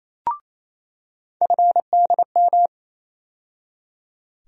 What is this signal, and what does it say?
Morse code, heard as a keyed tone, about 28 wpm, 700 Hz. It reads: FDM
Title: USA Call Sign Postfixes - 3 Letters 28wpm